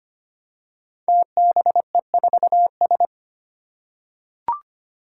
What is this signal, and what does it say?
Morse code, heard as a keyed tone, about 25 wpm, 700 Hz. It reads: TBE4S
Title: Five Challenging Letter-Number Combinations 25wpm